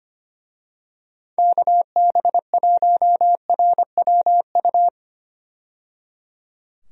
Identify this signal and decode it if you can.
Morse code, heard as a keyed tone, about 25 wpm, 700 Hz. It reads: KB1RWU
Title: USA Call Signs - 6 Characters - 25wpm